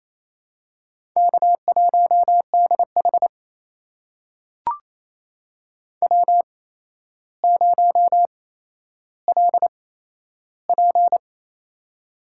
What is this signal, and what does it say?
Morse code, heard as a keyed tone, about 28 wpm, spaced as for 12 wpm, 700 Hz. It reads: K1DH W0LP
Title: USA Call Signs - 4 Characters - 8x Character Spacing - 28wpm